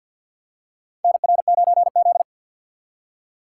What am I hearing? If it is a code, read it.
Morse code, heard as a keyed tone, about 46 wpm, 700 Hz. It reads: NR9Z
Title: USA Call Signs - 4 Characters - 3x Character Spacing - 50wpm